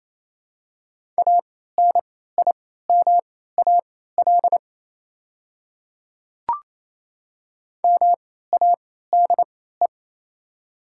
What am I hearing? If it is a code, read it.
Morse code, heard as a keyed tone, about 28 wpm, spaced as for 16 wpm, 700 Hz. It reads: ANIMAL MADE
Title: 3x Character Spacing - Top 300 Words - 28wpm